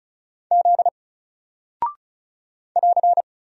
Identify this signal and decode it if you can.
Morse code, heard as a keyed tone, about 36 wpm, 700 Hz. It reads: Z <AR>
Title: Single Character (Letter:Number:Punctuation:Prosign) 35wpm